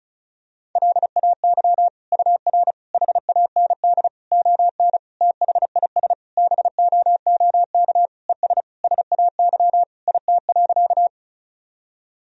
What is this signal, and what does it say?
Morse code, heard as a keyed tone, about 34 wpm, 700 Hz. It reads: LAY UR HAND ON THIS BOOK ES SAY IT.
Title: Sentences from Top 400 Words and Abbreviations 35wpm